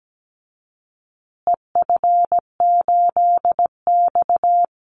Morse code, heard as a keyed tone, about 17 wpm, 700 Hz. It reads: EF8X
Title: Call Signs Repeated 2x - 17wpm